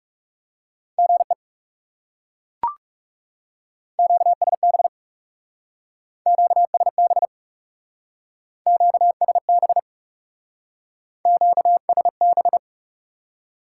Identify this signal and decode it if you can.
Morse code, heard as a keyed tone, about 37 wpm, 700 Hz. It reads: GE QSB QSB QSB QSB
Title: Speed-Racing Top 90 QSO Elements 30wpm